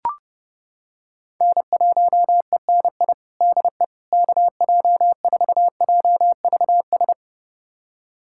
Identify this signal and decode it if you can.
Morse code, heard as a keyed tone, about 32 wpm, 700 Hz. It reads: N1ENI DE KJ4JVS
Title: USA Callsign DE Callsign 30wpm